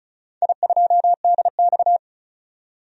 Morse code, heard as a keyed tone, about 36 wpm, 700 Hz. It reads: I2DX
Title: DXCC-100 Call Signs - 4 Characters - 35wpm